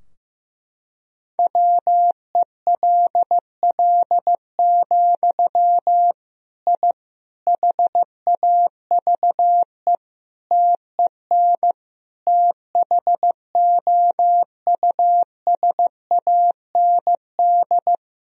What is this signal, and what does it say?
Morse code, heard as a keyed tone, about 16 wpm, 700 Hz. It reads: WELL, I HAVE TEN THOUSAND
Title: Sentences from Top 1000 Words - Review Part 3 - 15wpm